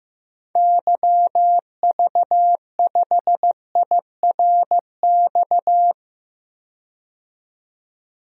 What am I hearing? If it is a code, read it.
Morse code, heard as a keyed tone, about 16 wpm, 700 Hz. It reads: YV5IRX
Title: DXCC-100 Call Signs - 6 Characters - 15wpm